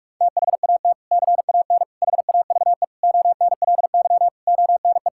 Morse code, heard as a keyed tone, about 46 wpm, 700 Hz. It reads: THAT CAN HAVE ONLY ONE
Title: Sentences from Top 1000 Words - Review Part 6 - 45wpm